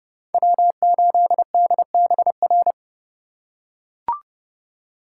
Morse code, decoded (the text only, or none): W8DBR